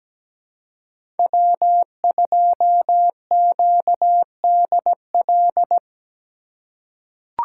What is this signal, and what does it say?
Morse code, heard as a keyed tone, about 17 wpm, 700 Hz. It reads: W2QDL
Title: USA Call Signs - 5 Characters - 2x Character Spacing - 17wpm